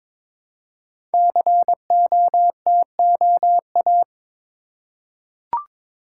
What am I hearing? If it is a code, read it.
Morse code, heard as a keyed tone, about 22 wpm, 700 Hz. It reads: COTOA
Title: Code Groups - 5 Letters - 22wpm